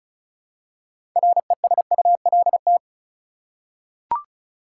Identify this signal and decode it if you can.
Morse code, heard as a keyed tone, about 36 wpm, 700 Hz. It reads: RESULT
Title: Six Letter Words - 35wpm